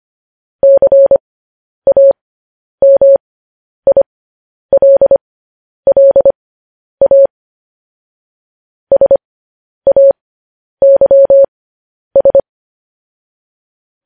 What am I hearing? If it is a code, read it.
Morse code, heard as a keyed tone, about 25 wpm, spaced as for 10 wpm, 550 Hz. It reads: CAMILLA SAYS
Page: news headlines in morse code at 10 wpm